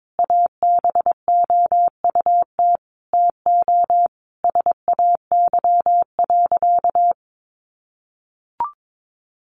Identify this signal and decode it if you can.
Morse code, heard as a keyed tone, about 22 wpm, 700 Hz. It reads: ABOUT TO SAY.
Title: Sentences from Top 1200 Words 22wpm